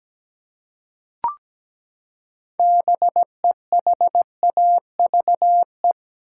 Morse code, heard as a keyed tone, about 17 wpm, 700 Hz. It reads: BEHAVE
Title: Six Letter Words - 17wpm